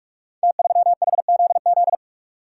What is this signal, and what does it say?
Morse code, heard as a keyed tone, about 44 wpm, 700 Hz. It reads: T3HZZ
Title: Code Groups - 5 Letter:Numbers - 45wpm